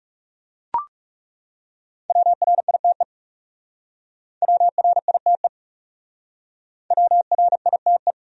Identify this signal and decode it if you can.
Morse code, heard as a keyed tone, about 38 wpm, 700 Hz. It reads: WRITE WRITE WRITE